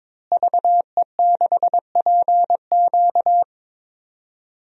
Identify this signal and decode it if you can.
Morse code, heard as a keyed tone, about 22 wpm, 700 Hz. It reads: VE6PQ
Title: DXCC-100 Call Signs - 5 Characters - Repeated 3x - 22wpm